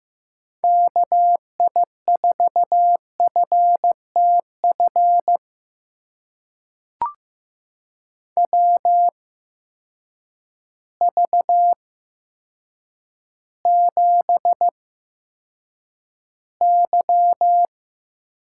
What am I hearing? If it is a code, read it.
Morse code, heard as a keyed tone, about 15 wpm, spaced as for 8 wpm, 700 Hz. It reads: KI4FTF WV7Y
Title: USA Call Signs - 8x Character Spacing - 15wpm